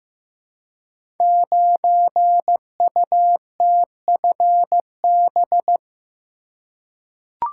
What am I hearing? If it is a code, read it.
Morse code, heard as a keyed tone, about 15 wpm, 700 Hz. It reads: 9UTFB
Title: Code Groups - 5 Letter:Numbers - 15wpm